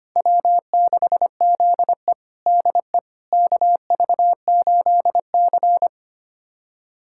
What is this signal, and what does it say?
Morse code, heard as a keyed tone, about 25 wpm, 700 Hz. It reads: W6ZE DE KV8C
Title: USA Callsign DE Callsign - 4 Characters - 25wpm